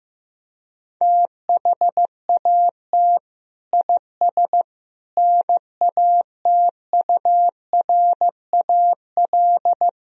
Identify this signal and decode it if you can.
Morse code, heard as a keyed tone, about 15 wpm, 700 Hz. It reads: THAT IS NATURAL
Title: Sentences from Top 1000 Words - Review Part 4 - 15wpm